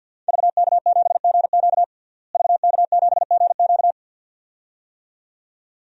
Morse code, heard as a keyed tone, about 51 wpm, 700 Hz. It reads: VK7GQ VK7GQ
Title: DXCC-100 Call Signs - 5 Characters - Repeated 2x - 50wpm